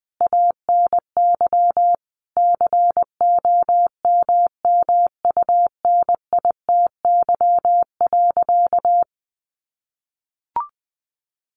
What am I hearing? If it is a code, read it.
Morse code, heard as a keyed tone, about 20 wpm, 700 Hz. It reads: ANY COMMUNITY.
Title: Sentences from Top 1200 Words 20wpm